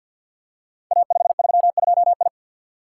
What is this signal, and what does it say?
Morse code, heard as a keyed tone, about 51 wpm, 700 Hz. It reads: A532I
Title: Code Groups - 5 Letter:Numbers - 50wpm